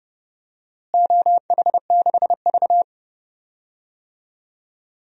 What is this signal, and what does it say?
Morse code, heard as a keyed tone, about 30 wpm, 700 Hz. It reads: OH6V
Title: DXCC-100 Call Signs - 4 Characters - Repeated 2x - 30wpm